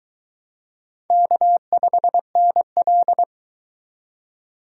Morse code, heard as a keyed tone, about 24 wpm, 700 Hz. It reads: K5NL